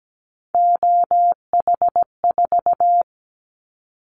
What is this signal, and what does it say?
Morse code, heard as a keyed tone, about 17 wpm, 700 Hz. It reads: OH4
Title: Callsign Prefixes for Top 50 DXCC - 17wpm